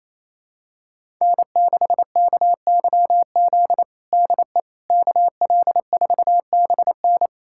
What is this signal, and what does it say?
Morse code, heard as a keyed tone, about 27 wpm, 700 Hz. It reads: N6KYZ DE KL4BN
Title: USA Callsign DE Callsign - 5 Characters - 28wpm